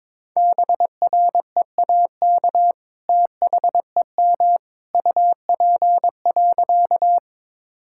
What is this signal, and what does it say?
Morse code, heard as a keyed tone, about 22 wpm, 700 Hz. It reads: BREAK THEM UP.
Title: Sentences from Top 1000 Words - Review Part 6 - 22wpm